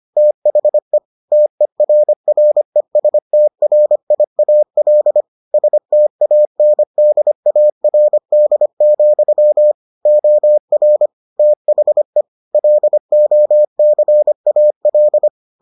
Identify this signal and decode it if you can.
Morse code, heard as a keyed tone, about 25 wpm, 600 Hz. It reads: THE TERRESTRIAL STANDARD, OR THE LOCAL